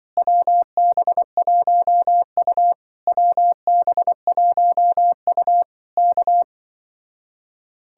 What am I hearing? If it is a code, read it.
Morse code, heard as a keyed tone, about 25 wpm, 700 Hz. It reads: WB1U WB1U K